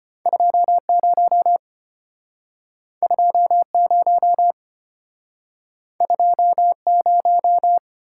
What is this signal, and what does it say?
Morse code, heard as a keyed tone, about 35 wpm, 700 Hz. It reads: 20 20 20